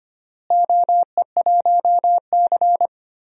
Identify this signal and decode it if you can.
Morse code, heard as a keyed tone, about 25 wpm, 700 Hz. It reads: OE1C
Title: DXCC-100 Call Signs - 4 Characters - Repeated 2x - 25wpm